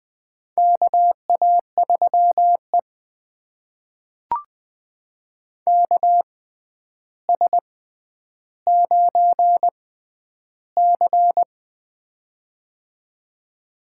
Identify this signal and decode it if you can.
Morse code, heard as a keyed tone, about 20 wpm, spaced as for 10 wpm, 700 Hz. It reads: KA3E KS9C
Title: USA Call Signs - 4 Characters - 6x Character Spacing - 20wpm